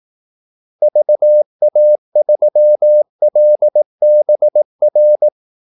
Morse code, heard as a keyed tone, about 18 wpm, 600 Hz. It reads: VA3LBR
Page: Kaldesignaler 18 wpm